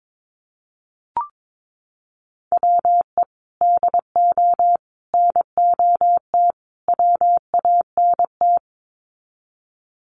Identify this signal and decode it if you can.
Morse code, heard as a keyed tone, about 22 wpm, 700 Hz. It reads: WE DO NOT WANT
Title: Sets of 4 Words 22wpm